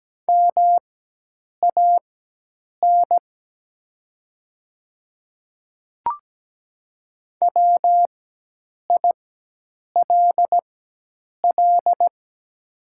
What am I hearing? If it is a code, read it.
Morse code, heard as a keyed tone, about 17 wpm, spaced as for 8 wpm, 700 Hz. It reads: MAN WILL
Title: 4x Character Spacing - Top 300 Words - 17wpm